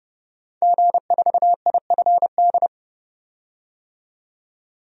Morse code, heard as a keyed tone, about 32 wpm, 700 Hz. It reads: G4IFD
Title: DXCC-100 Call Signs - 5 Characters - 30wpm